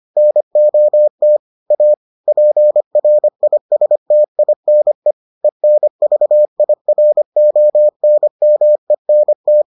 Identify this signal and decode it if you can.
Morse code, heard as a keyed tone, about 25 wpm, 600 Hz. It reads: NOT A PRISTINE ENVIRONMENT